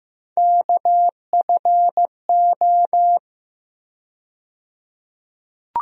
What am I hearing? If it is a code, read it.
Morse code, heard as a keyed tone, about 15 wpm, 700 Hz. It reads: KFO
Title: USA Call Sign Postfixes - 3 Letters 15wpm